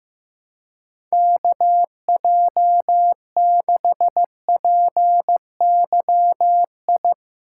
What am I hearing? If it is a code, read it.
Morse code, heard as a keyed tone, about 16 wpm, 700 Hz. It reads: KJ6PYI
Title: USA Callsign DE Callsign - 6 Characters - 15wpm